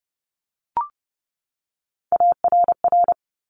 Morse code, heard as a keyed tone, about 30 wpm, 690 Hz. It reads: ARR
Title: Top Three Letter Combinations in Words 30wpm